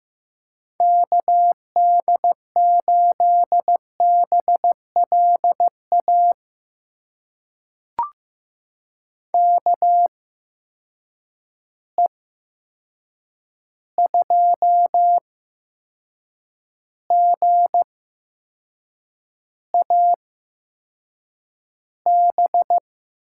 Morse code, heard as a keyed tone, about 15 wpm, spaced as for 6 wpm, 700 Hz. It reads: KD8BLA KE2GAB
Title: USA Call Signs - 6 Characters - 8x Character Spacing - 15wpm